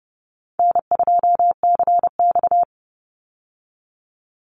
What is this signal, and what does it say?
Morse code, heard as a keyed tone, about 30 wpm, 700 Hz. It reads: N2CX
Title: Call Signs - 30wpm